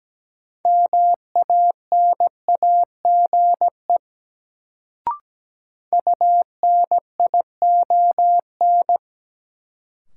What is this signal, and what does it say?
Morse code, heard as a keyed tone, about 17 wpm, 700 Hz. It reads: MANAGE UNION
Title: Sentences from Top 1400 Words - New Word Drill - 17wpm